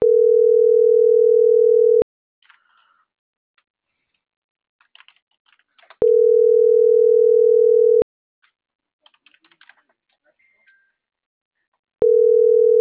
Ringback tone (North America), 440 + 480 Hz.
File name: CENTRAL MABE / out maria - .66866